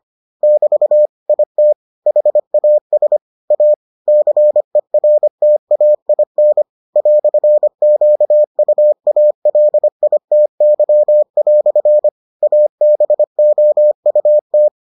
Morse code, heard as a keyed tone, about 25 wpm, 600 Hz. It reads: <BT> IT HAS A CERTAIN "QUALITY" ABOUT